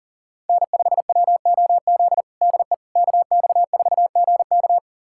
Morse code, heard as a keyed tone, about 40 wpm, 700 Hz. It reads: N5WOZ DE KX4GK
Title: USA Callsign DE Callsign - 5 Characters - 40wpm